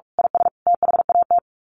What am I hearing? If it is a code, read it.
Morse code, heard as a keyed tone, about 44 wpm, 700 Hz. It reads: IS THAT
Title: Sets of 5 Words Encore 45wpm